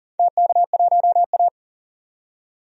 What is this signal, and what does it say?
Morse code, heard as a keyed tone, about 40 wpm, 700 Hz. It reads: TK1A
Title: DXCC-100 Call Signs - 4 Characters - 40wpm